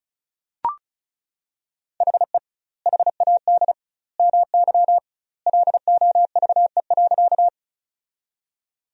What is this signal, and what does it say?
Morse code, heard as a keyed tone, about 34 wpm, 700 Hz. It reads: HE HAD MY LOVE.
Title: 2x Word-Spacing - Sentences from Top 400 Words - 35wpm